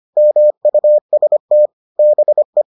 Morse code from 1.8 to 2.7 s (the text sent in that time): BE